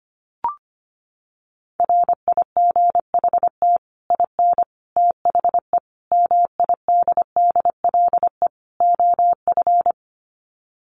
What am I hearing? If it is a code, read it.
Morse code, heard as a keyed tone, about 25 wpm, 700 Hz. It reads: RIGHT IN THE MIDDLE OF
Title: Sets of 5 Words 25wpm